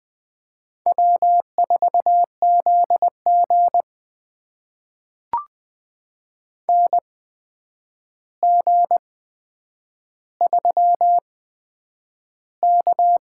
Morse code, heard as a keyed tone, about 20 wpm, spaced as for 9 wpm, 700 Hz. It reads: W4ZG NG3K
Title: USA Call Signs - 4 Characters - 8x Character Spacing - 20wpm